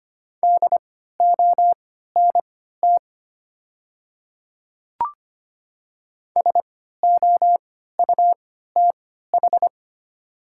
Morse code, heard as a keyed tone, about 25 wpm, spaced as for 14 wpm, 700 Hz. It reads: DONT SOUTH
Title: 3x Character Spacing - Top 500 Words - 25wpm